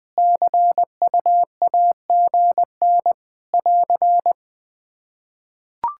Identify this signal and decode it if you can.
Morse code, heard as a keyed tone, about 20 wpm, 700 Hz. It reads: CUAGN <AR>
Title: K1USN SST QSO Practice 20wpm